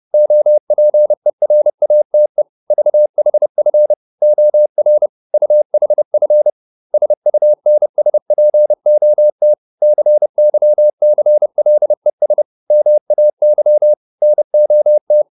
Morse code, heard as a keyed tone, about 30 wpm, 600 Hz. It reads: OPERATE VHF OR UHF SUNSPOT CYCLES MAY NOT